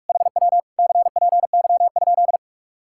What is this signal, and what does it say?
Morse code, heard as a keyed tone, about 46 wpm, 700 Hz. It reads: HW CPY?
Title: Speed-Racing Common QSO Phrases - 45wpm